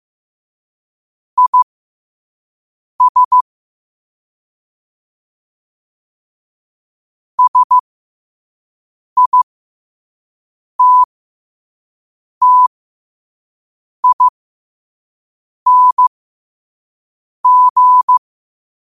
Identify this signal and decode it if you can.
Morse code, heard as a keyed tone, about 15 wpm, spaced as for 5 wpm, 1000 Hz. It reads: IS SITTING
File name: qotdfiles / 5wpm